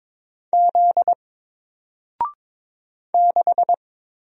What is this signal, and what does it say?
Morse code, heard as a keyed tone, about 22 wpm, 700 Hz. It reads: Z 6
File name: Single-Letter-Number-22wpm